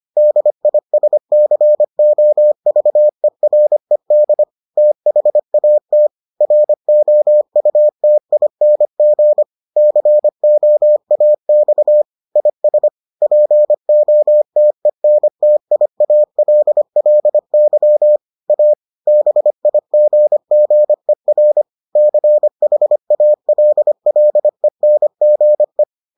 Morse code, heard as a keyed tone, about 25 wpm, 600 Hz. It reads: DISCOVERED THAT ROUTING COAX IS POTENTIALLY A BIGGER CHALLENGE